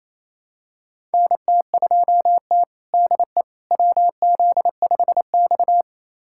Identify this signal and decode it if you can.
Morse code, heard as a keyed tone, about 28 wpm, 700 Hz. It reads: NT2T DE WZ5X
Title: USA Callsign DE Callsign - 4 Characters - 28wpm